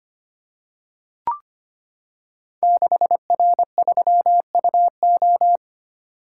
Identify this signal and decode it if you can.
Morse code, heard as a keyed tone, about 25 wpm, 700 Hz. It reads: 6R3UO
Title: Code Groups - 5 Letter:Numbers - 25wpm